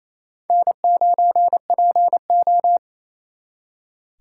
Morse code, heard as a keyed tone, about 34 wpm, 700 Hz. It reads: N9PO
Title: Speed Racing - USA 1x2 Call Signs 28wpm